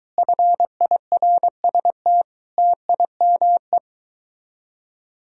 Morse code, heard as a keyed tone, about 23 wpm, 700 Hz. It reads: FIRST TIME